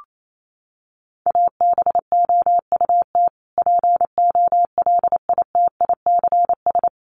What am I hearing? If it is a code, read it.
Morse code, heard as a keyed tone, about 28 wpm, 700 Hz. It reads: ABOUT POLITICS
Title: Sentences from Top 1000 Words 28wpm